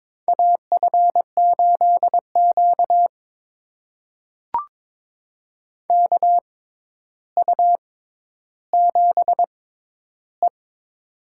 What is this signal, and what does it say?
Morse code, heard as a keyed tone, about 22 wpm, spaced as for 11 wpm, 700 Hz. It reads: AF8Q KU7E
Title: USA Call Signs - 4 Characters - 6x Character Spacing - 22wpm